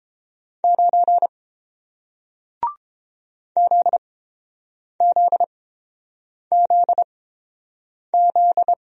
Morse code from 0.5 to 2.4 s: 9

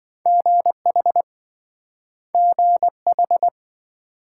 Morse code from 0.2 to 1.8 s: GH

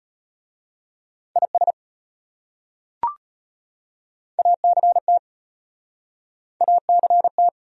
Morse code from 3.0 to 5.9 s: ACT

